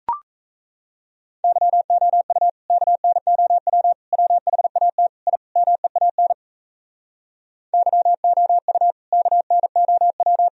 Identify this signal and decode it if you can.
Morse code, heard as a keyed tone, about 38 wpm, 700 Hz. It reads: YOU KNOW WHAT I MEAN YOU KNOW